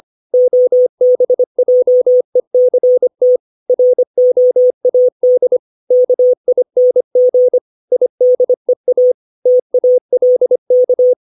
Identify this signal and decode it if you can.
Morse code, heard as a keyed tone, about 25 wpm, 500 Hz. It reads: OBJECT ROAD KING IDEA TALK
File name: 632 words / 632 Words 25WPM